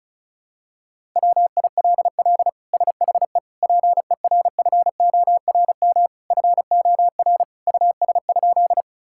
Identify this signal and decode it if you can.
Morse code, heard as a keyed tone, about 36 wpm, 700 Hz. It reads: WILL SHE PERFORM FOR US?
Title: Sentences from Top 1000 Words - Review Part 5 - 35wpm